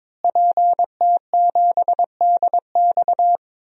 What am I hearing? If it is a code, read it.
Morse code, heard as a keyed tone, about 22 wpm, 700 Hz. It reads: PT7DX
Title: DXCC-100 Call Signs - 5 Characters - Repeated 3x - 22wpm